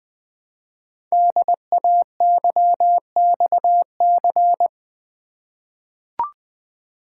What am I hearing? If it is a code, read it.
Morse code, heard as a keyed tone, about 20 wpm, 700 Hz. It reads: DAYXC
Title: Code Groups - 5 Letters - 20wpm